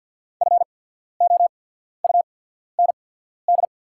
Morse code, heard as a keyed tone, about 50 wpm, spaced as for 14 wpm, 690 Hz. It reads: FOUND
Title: 8x Character Spacing - Top 500 Words - 50wpm